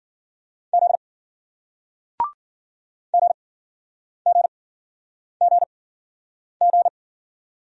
Morse code, heard as a keyed tone, about 54 wpm, spaced as for 9 wpm, 700 Hz. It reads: Z GGGG